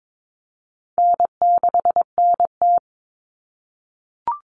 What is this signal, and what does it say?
Morse code, heard as a keyed tone, about 22 wpm, 700 Hz. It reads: N6NT
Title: Common Cut Numbers - 4-Digit 22wpm